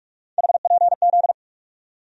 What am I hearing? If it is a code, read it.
Morse code, heard as a keyed tone, about 42 wpm, 700 Hz. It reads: HPZ
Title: USA Call Sign Postfixes - 3 Letters 45wpm